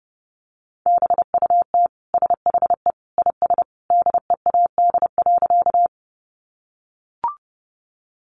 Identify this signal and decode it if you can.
Morse code, heard as a keyed tone, about 30 wpm, 700 Hz. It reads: BUT SHE IS DEAD.
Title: Sentences from Top 1500 Words 30wpm